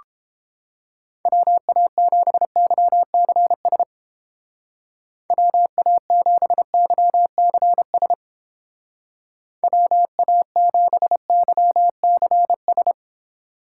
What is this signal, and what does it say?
Morse code, heard as a keyed tone, about 30 wpm, 700 Hz. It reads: WA7YCS WA7YCS WA7YCS